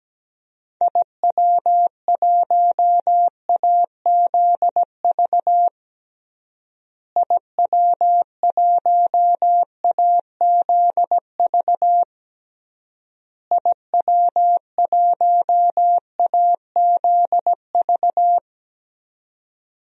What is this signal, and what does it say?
Morse code, heard as a keyed tone, about 17 wpm, 700 Hz. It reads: IW1AZV IW1AZV IW1AZV
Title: DXCC-100 Call Signs - 6 Characters - Repeated 3x - 17wpm